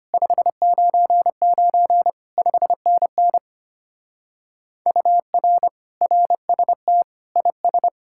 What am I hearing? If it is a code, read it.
Morse code, heard as a keyed tone, about 32 wpm, 700 Hz. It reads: 599 5NN UR RST IS